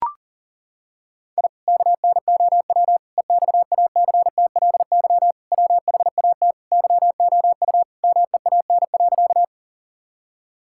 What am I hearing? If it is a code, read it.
Morse code, heard as a keyed tone, about 37 wpm, 700 Hz. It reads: I KNOW EXACTLY WHAT YOU MEAN.
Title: Sentences from Top 1000 Words - Review Part 6 - 40wpm